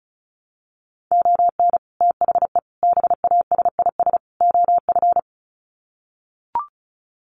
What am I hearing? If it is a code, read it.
Morse code, heard as a keyed tone, about 34 wpm, 700 Hz. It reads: ON THE BASIS OF
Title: Sets of 4 Words 35wpm